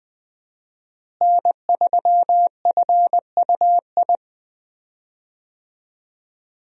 Morse code, heard as a keyed tone, about 20 wpm, 700 Hz. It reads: N3FUI
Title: USA Call Signs - 5 Characters - Repeated 2x - 20wpm